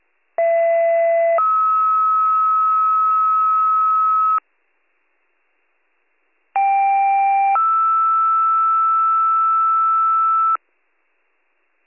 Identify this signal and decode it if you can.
Two-tone page, heard as a one-second tone, then a three-second tone, about 670 then 1290 Hz, about 790 then 1320 Hz. It is heard from 0.4 to 10.6 s.